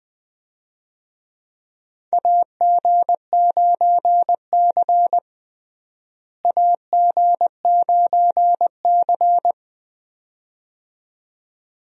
Morse code, heard as a keyed tone, about 20 wpm, 700 Hz. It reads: AG9C AG9C
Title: USA Call Signs - 4 Characters - Repeated 2x - 20wpm